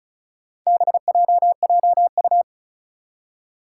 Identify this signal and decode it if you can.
Morse code, heard as a keyed tone, about 36 wpm, 700 Hz. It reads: BJJU
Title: Four Challenging Letter-Number Combinations 35wpm